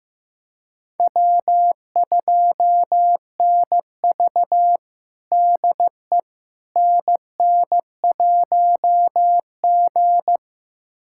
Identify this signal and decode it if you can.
Morse code, heard as a keyed tone, about 15 wpm, 700 Hz. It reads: W2NV DE NN1G
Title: USA Callsign DE Callsign - 4 Characters - 15wpm